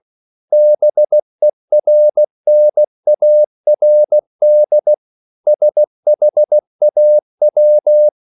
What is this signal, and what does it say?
Morse code, heard as a keyed tone, about 16 wpm, 600 Hz. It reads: BERNARD SHAW
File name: CW / Practice / EnglishQuotations0116